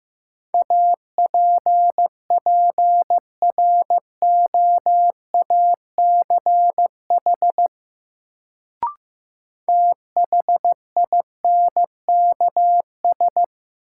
Morse code, heard as a keyed tone, about 15 wpm, 700 Hz. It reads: APPROACH THINKS
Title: Sentences from Top 1000 Words - Review Part 6 - New Word Drill - 15wpm